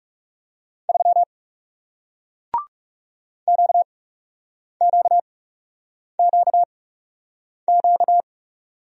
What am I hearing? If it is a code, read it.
Morse code, heard as a keyed tone, about 40 wpm, spaced as for 8 wpm, 700 Hz. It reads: 3 QQQQ